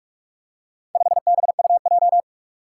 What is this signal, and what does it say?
Morse code, heard as a keyed tone, about 41 wpm, 700 Hz. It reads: 5BUJ
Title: Four Challenging Letter-Number Combinations 45wpm